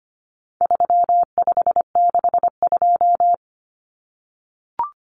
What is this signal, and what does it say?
Morse code, heard as a keyed tone, about 25 wpm, 700 Hz. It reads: 3562
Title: Numbers - 4-Digit 25wpm